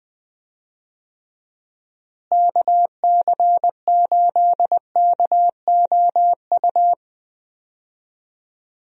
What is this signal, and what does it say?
Morse code, heard as a keyed tone, about 20 wpm, 700 Hz. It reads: KC8KOU
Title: USA Call Signs - 6 Characters - Repeated 3x - 20wpm